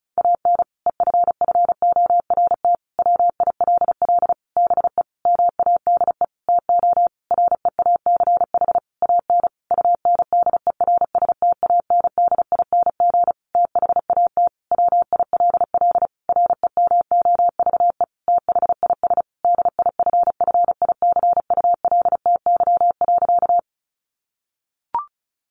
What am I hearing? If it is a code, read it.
Morse code, heard as a keyed tone, about 34 wpm, 700 Hz. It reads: AN EFFORT WILL BE MADE TO REACH AN UNDERSTANDING THAT WILL REMOVE THIS DIFFICULTY.
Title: Sentences from Top 1500 Words 35wpm